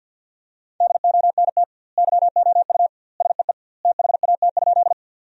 Morse code, heard as a keyed tone, about 50 wpm, 700 Hz. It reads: DONT YOU SEE THAT?
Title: 2x Word-Spacing - Sentences from Top 300 Words - 50wpm